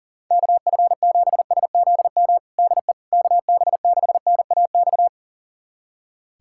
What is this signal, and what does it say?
Morse code, heard as a keyed tone, about 40 wpm, 700 Hz. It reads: KF7SZM DE KB6NAX
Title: USA Callsign DE Callsign - 6 Characters - 40wpm